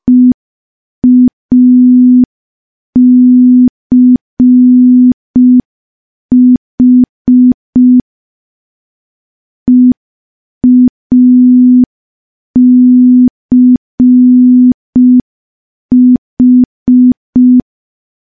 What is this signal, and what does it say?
Morse code, heard as a keyed tone, about 5 wpm, 250 Hz. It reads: EACH EACH
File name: morse / morse-training-05WPM